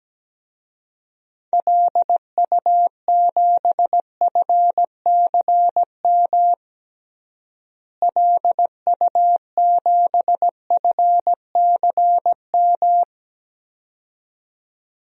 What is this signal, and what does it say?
Morse code, heard as a keyed tone, about 17 wpm, 700 Hz. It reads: LU7FCM LU7FCM
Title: DXCC-100 Call Signs - 6 Characters - Repeated 2x - 17wpm